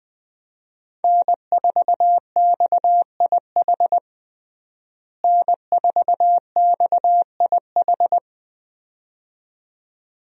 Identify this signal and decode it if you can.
Morse code, heard as a keyed tone, about 20 wpm, 700 Hz. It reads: N4XIH N4XIH
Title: USA Call Signs - 5 Characters - Repeated 2x - 20wpm